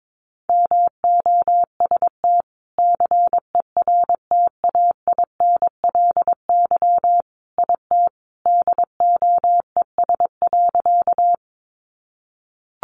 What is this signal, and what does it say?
Morse code, heard as a keyed tone, about 22 wpm, 700 Hz. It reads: MOST CERTAINLY IT DOES.
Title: Sentences from Top 900 Words 22wpm